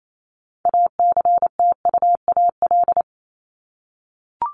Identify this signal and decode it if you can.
Morse code, heard as a keyed tone, about 28 wpm, 700 Hz. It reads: ACTUAL
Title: LY Suffix - Top 100 Words with Base Words 28wpm